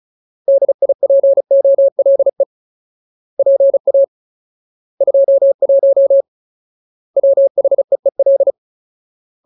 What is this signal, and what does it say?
Morse code, heard as a keyed tone, about 34 wpm, 550 Hz. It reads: DIPOLE PA 21 WHEEL
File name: Ultimate Word List / 35WPM Ultimate